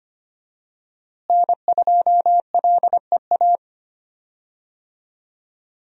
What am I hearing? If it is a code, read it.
Morse code, heard as a keyed tone, about 25 wpm, 700 Hz. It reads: N2LEA
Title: USA Call Signs - 5 Characters - Repeated 2x - 25wpm